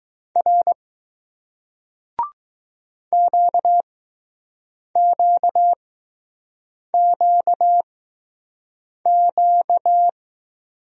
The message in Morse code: R QQQQ

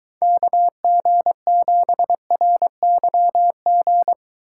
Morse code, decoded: KG7RYG

